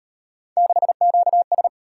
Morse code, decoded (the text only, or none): 6QS